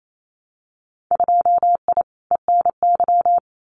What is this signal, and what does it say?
Morse code, heard as a keyed tone, about 28 wpm, 700 Hz. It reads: 2I ENY
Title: Winter Field Day Exchanges Encore 28wpm